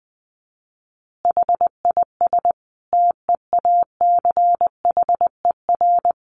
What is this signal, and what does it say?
Morse code, heard as a keyed tone, about 20 wpm, 700 Hz. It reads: HIS TEACHER
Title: Additional Easy Phrases 20wpm